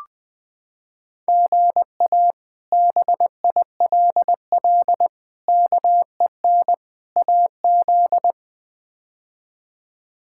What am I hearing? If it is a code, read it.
Morse code, heard as a keyed tone, about 20 wpm, 700 Hz. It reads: GA BILL KEN AZ
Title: K1USN SST QSO Practice - 20wpm